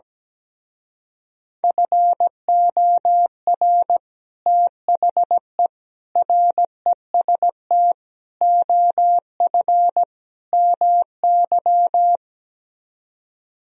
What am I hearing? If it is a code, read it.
Morse code, heard as a keyed tone, about 17 wpm, 700 Hz. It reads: FOR THE REST OF MY
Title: Speed-Racing - Sets of 5 Words 17wpm